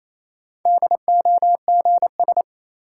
Morse code, heard as a keyed tone, about 28 wpm, 700 Hz. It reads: DOGS
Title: Sentences from Top 1600 Words - New Word Drill - 28wpm